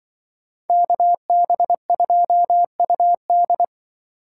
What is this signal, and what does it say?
Morse code, heard as a keyed tone, about 24 wpm, 700 Hz. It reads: KB2UD
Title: Speed Racing - USA Call Signs 20wpm